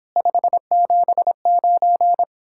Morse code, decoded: 579

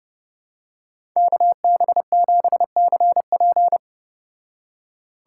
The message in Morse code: KB7CP

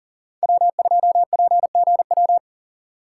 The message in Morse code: W2PGW

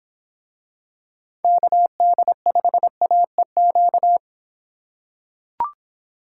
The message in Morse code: KD5AEQ

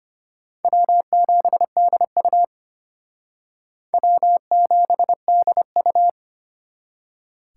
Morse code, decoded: W7DU W7DU

